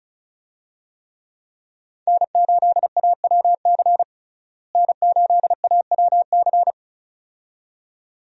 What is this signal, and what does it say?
Morse code, heard as a keyed tone, about 36 wpm, 700 Hz. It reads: N8AWC N8AWC
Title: USA Call Signs - 5 Characters - Repeated 2x - 35wpm